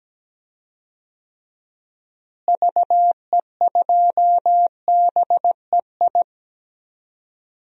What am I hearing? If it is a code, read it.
Morse code, heard as a keyed tone, about 17 wpm, 700 Hz. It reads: VE2BEI
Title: DXCC-100 Call Signs - 6 Characters - Repeated 3x - 17wpm